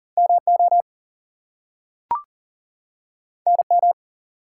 Morse code, heard as a keyed tone, about 40 wpm, 700 Hz. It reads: MO NM
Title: US States and Canadian Provinces 40wpm